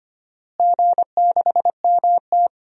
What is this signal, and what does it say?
Morse code, heard as a keyed tone, about 25 wpm, 700 Hz. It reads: G6MT
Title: DXCC-100 Call Signs - 4 Characters - Repeated 2x - 25wpm